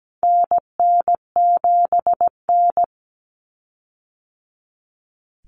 Morse code, heard as a keyed tone, about 17 wpm, 700 Hz. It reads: NN7N
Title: Common Cut Numbers - 4-Digit 17wpm